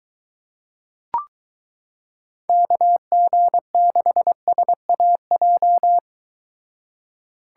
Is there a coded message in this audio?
Yes: Morse code from 2.5 to 6.0 s, heard as a keyed tone, about 23 wpm, 700 Hz. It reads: KG6SAJ